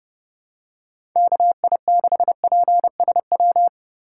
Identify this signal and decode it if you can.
Morse code, heard as a keyed tone, about 30 wpm, 700 Hz. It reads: KI6PSW
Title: USA Call Signs - 6 Characters - 2x Character Spacing - 30wpm